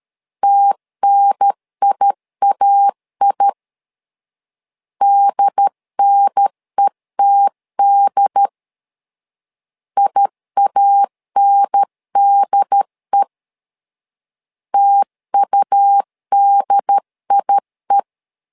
Morse code, heard as a keyed tone, about 13 wpm, 800 Hz. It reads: TNIAI DNETD IANDE TUDIE
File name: LEZIONI / 2 UHDN / 2 UHDN 12wpm 1456